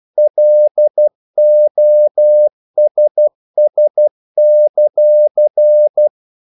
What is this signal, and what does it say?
Morse code, heard as a keyed tone, about 12 wpm, 600 Hz. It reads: LOSS;
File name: CW / Practice / If12